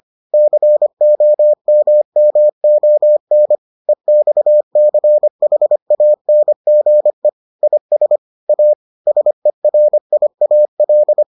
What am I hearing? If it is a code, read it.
Morse code, heard as a keyed tone, about 25 wpm, 600 Hz. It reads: COMMON EXCHANGE IS A SERIAL